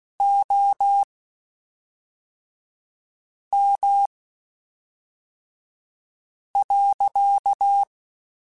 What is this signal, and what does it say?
Morse code, heard as a keyed tone, about 16 wpm, spaced as for 3 wpm, 800 Hz. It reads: OM.
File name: (spaced) / QSOs Spaced-mp3s / kc7haa @5 sp